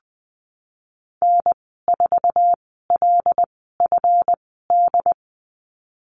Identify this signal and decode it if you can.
Morse code, heard as a keyed tone, about 19 wpm, spaced as for 14 wpm, 700 Hz. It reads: N4LFD
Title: 154 Lesson 25g - Letter p - Call Signs 20wpm